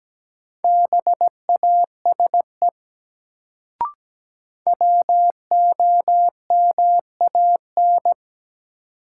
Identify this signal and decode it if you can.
Morse code, heard as a keyed tone, about 17 wpm, 700 Hz. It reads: BASE WOMAN
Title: Sets of 5 Words - New Word Drill - 17wpm